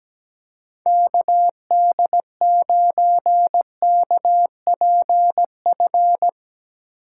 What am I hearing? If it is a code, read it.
Morse code, heard as a keyed tone, about 17 wpm, 700 Hz. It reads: KD9KPF
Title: USA Call Signs - 2x Character Spacing - 17wpm